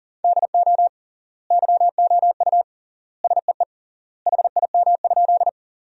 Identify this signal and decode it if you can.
Morse code, heard as a keyed tone, about 40 wpm, 700 Hz. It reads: DO YOU SEE HIM?
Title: 3x Word-Spacing - Sentences from Top 100 Words - 40wpm